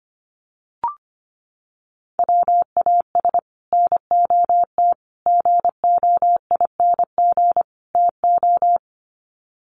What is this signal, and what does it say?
Morse code, heard as a keyed tone, about 25 wpm, 700 Hz. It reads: WAS NOT GOING TO
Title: Sets of 4 Words 25wpm